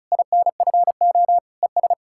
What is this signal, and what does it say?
Morse code, heard as a keyed tone, about 47 wpm, 700 Hz. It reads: INFO ES